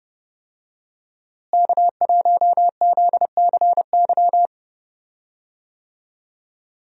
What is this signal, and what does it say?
Morse code, heard as a keyed tone, about 30 wpm, 700 Hz. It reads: K1ZCY
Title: USA Call Signs - 5 Characters - Repeated 3x - 30wpm